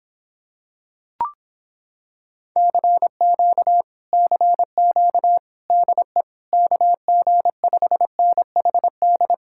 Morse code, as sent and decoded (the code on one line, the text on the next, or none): -.-. --.- / -.-. --.- / -.. / -.- --. ..... -. .... -..
CQ CQ DE KG5NHD